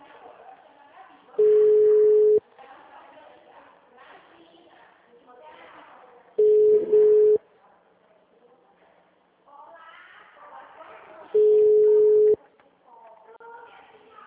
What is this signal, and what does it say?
Ringback tone (Europe, 425 Hz), with speech.
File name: out- -2007- - - .79900